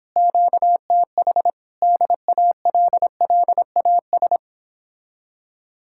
Morse code, heard as a keyed tone, about 31 wpm, 700 Hz. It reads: QTH DALLAS